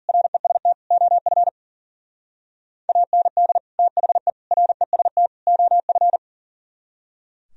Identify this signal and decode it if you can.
Morse code, heard as a keyed tone, about 49 wpm, 700 Hz. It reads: REST OF AND THE REST OF
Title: Speed-Racing - Sets of 4 Words 40wpm